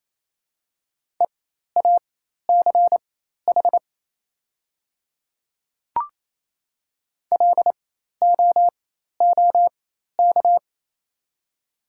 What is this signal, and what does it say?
Morse code, heard as a keyed tone, about 27 wpm, spaced as for 13 wpm, 700 Hz. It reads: EACH LOOK
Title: 4x Character Spacing - Top 100 Words - 28wpm